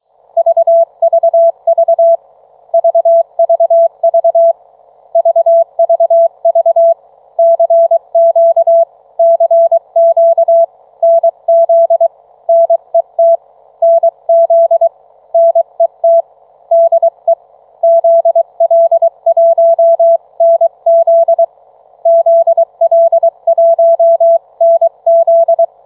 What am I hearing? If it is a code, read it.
Morse code, heard as a keyed tone, about 23 wpm, 660 Hz. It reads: VVV VVV VVV CQ CQ NZ NET NZ NET DE ZL1NZ ZL1NZ